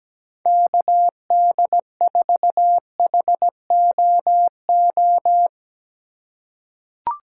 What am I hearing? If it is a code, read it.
Morse code, heard as a keyed tone, about 17 wpm, 700 Hz. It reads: KD4HOO
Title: USA Call Signs - 2x Character Spacing - 17wpm